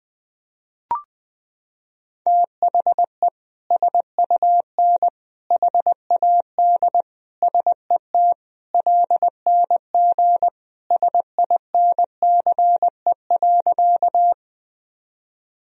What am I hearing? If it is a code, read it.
Morse code, heard as a keyed tone, about 21 wpm, 700 Hz. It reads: THE SUN HAD SET LNG SINCE.
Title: Sentences from Top 300 Words and Abbreviations 20wpm